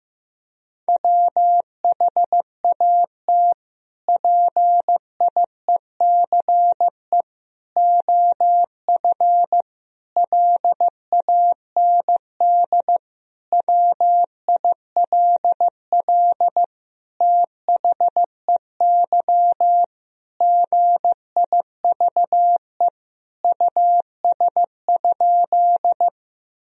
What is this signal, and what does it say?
Morse code, heard as a keyed tone, about 15 wpm, 700 Hz. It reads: WHAT PIECE OF LAND WILL THEY GIVE US?
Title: Sentences from Top 1000 Words - Review Part 2 - 15wpm